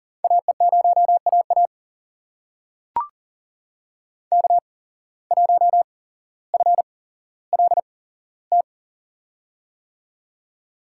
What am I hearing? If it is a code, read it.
Morse code, heard as a keyed tone, about 39 wpm, spaced as for 17 wpm, 700 Hz. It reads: AE0AA K1FLT
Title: USA Call Signs - 5 Characters - 8x Character Spacing - 40wpm